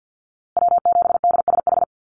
Morse code, heard as a keyed tone, about 50 wpm, 700 Hz. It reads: W7DSH
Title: USA Call Signs Repeated 3x 50wpm